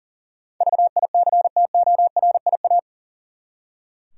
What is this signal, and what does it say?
Morse code, heard as a keyed tone, about 38 wpm, 700 Hz. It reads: VICTORIA
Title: USA Common Names - 40wpm